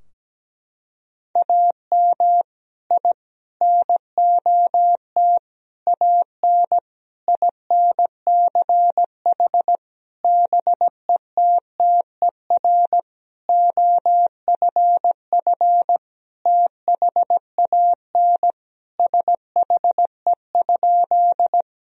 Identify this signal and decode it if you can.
Morse code, heard as a keyed tone, about 17 wpm, 700 Hz. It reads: AM I NOT AN INCH BETTER OFF THAN SHE?
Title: Sentences from Top 1000 Words - Review Part 3 - 17wpm